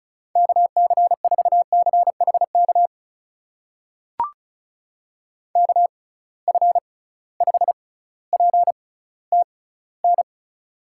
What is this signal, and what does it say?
Morse code, heard as a keyed tone, about 36 wpm, spaced as for 18 wpm, 700 Hz. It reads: KC4CHK KF5PTN
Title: USA Call Signs - 6 Characters - 6x Character Spacing - 35wpm